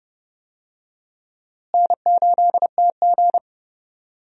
Morse code, heard as a keyed tone, about 32 wpm, 700 Hz. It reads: N8TG
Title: USA Call Signs - 4 Characters - Repeated 3x - 30wpm